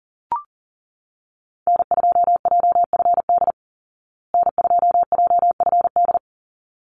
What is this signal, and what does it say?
Morse code, heard as a keyed tone, about 40 wpm, 700 Hz. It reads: N2JFD N2JFD
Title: Call Signs Repeated 3x - 40wpm